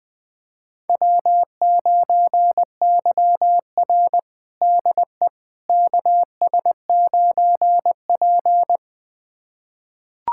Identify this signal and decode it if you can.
Morse code, heard as a keyed tone, about 20 wpm, 700 Hz. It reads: W9YR DE KS9P
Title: USA Callsign DE Callsign - 4 Characters - 20wpm